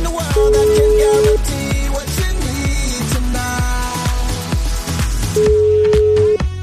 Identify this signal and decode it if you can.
Ringback tone (Europe, 425 Hz), with noise.